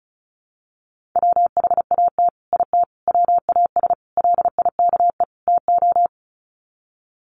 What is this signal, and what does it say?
Morse code, heard as a keyed tone, about 34 wpm, 700 Hz. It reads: WHAT IT WAS LIKE TO
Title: Sets of 5 Words Encore 35wpm